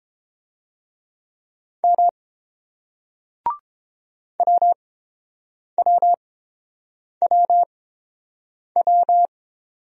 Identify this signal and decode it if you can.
Morse code, heard as a keyed tone, about 30 wpm, spaced as for 7 wpm, 700 Hz. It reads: M WWWW